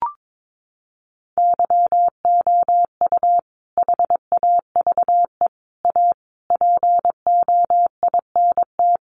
Morse code, heard as a keyed tone, about 22 wpm, 700 Hz. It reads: YOU HAVE A POINT
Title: Sentences from Top 1400 Words 22wpm